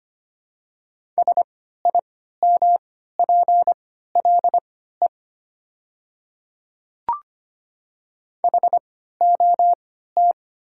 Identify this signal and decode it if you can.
Morse code, heard as a keyed tone, about 25 wpm, spaced as for 14 wpm, 700 Hz. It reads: SIMPLE HOT
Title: 3x Character Spacing - Top 500 Words - 25wpm